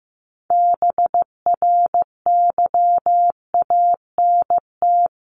Morse code, heard as a keyed tone, about 15 wpm, 700 Hz. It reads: BRYANT
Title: CWT Exchange 15wpm